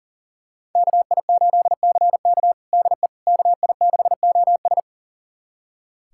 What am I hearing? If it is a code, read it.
Morse code, heard as a keyed tone, about 37 wpm, 700 Hz. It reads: KI8CK DE KI6OS
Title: USA Callsign DE Callsign - 5 Characters - 40wpm